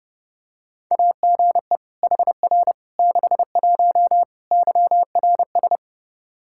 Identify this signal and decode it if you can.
Morse code, heard as a keyed tone, about 30 wpm, 700 Hz. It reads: AGE HR 61 YRS
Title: Practice QSO - Exchange 3 - More Info - 30wpm